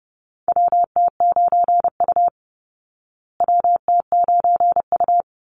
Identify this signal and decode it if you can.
Morse code, heard as a keyed tone, about 30 wpm, 700 Hz. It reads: WT9U WT9U
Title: Call Signs Repeated 3x - 30wpm